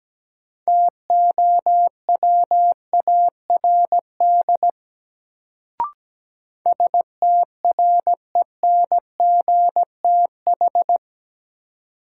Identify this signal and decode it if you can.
Morse code, heard as a keyed tone, about 17 wpm, 700 Hz. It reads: TOWARD STRENGTH
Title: Sentences from Top 1300 Words - New Word Drill Alt - 17wpm